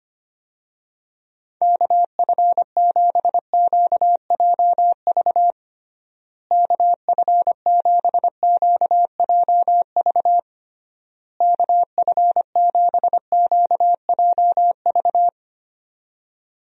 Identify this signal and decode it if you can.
Morse code, heard as a keyed tone, about 25 wpm, 700 Hz. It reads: KF7QJV KF7QJV KF7QJV
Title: USA Call Signs - 6 Characters - Repeated 3x - 25wpm